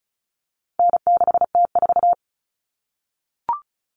Morse code, heard as a keyed tone, about 34 wpm, 700 Hz. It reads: N6T4
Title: All Cut Numbers - 4-Digit 35wpm